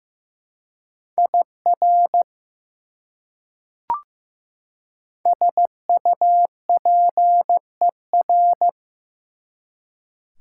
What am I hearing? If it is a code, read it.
Morse code, heard as a keyed tone, about 15 wpm, 700 Hz. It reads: IR SUPER
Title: Prefixes and Suffixes 15wpm